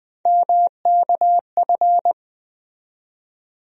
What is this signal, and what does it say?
Morse code, heard as a keyed tone, about 20 wpm, 700 Hz. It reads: MKF
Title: USA Call Sign Postfixes - 3 Letters 20wpm